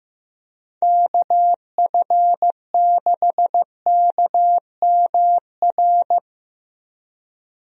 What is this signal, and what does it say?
Morse code, heard as a keyed tone, about 15 wpm, 700 Hz. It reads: KF6KMR
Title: USA Call Signs - 6 Characters - 15wpm